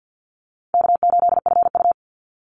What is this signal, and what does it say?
Morse code, heard as a keyed tone, about 49 wpm, 700 Hz. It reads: K8FU
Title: Easy Call Signs Repeated 2x - 50wpm